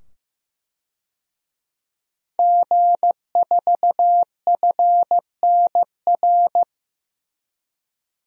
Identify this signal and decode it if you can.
Morse code, heard as a keyed tone, about 15 wpm, 700 Hz. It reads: G4FNR
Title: DXCC-100 Call Signs - 5 Characters - 15wpm